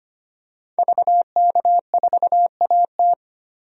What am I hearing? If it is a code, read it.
Morse code, heard as a keyed tone, about 25 wpm, 700 Hz. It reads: VK4AT
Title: DXCC-100 Call Signs - 5 Characters - 25wpm